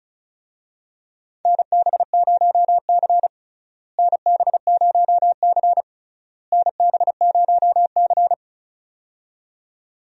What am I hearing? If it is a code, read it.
Morse code, heard as a keyed tone, about 36 wpm, 700 Hz. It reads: NB0C NB0C NB0C
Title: USA Call Signs - 4 Characters - Repeated 3x - 35wpm